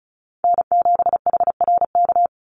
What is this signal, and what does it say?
Morse code, heard as a keyed tone, about 34 wpm, 700 Hz. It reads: N7HRK
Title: Call Signs - 35wpm